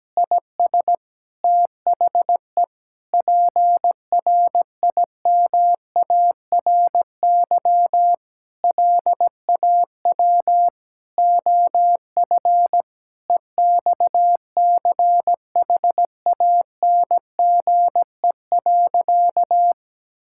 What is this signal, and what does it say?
Morse code, heard as a keyed tone, about 17 wpm, 700 Hz. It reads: IS THE PRIMARY LAW OF EXCHANGE.
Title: Sentences from Top 1000 Words - Review Part 6 - 17wpm